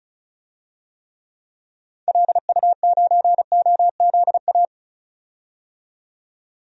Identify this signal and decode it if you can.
Morse code, heard as a keyed tone, about 34 wpm, 700 Hz. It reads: LU9OZA
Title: DXCC-100 Call Signs - 6 Characters - 35wpm